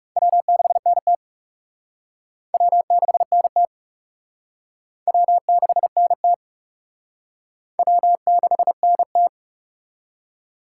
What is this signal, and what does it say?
Morse code, heard as a keyed tone, about 40 wpm, 700 Hz. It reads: W6NT W6NT W6NT W6NT